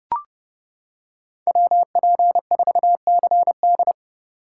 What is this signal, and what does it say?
Morse code, heard as a keyed tone, about 28 wpm, 700 Hz. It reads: WP4CD